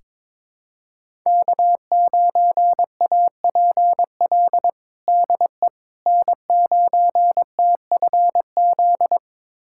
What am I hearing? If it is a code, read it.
Morse code, heard as a keyed tone, about 21 wpm, 700 Hz. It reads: K9APL DE N9TFZ
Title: USA Callsign DE Callsign - 5 Characters - 22wpm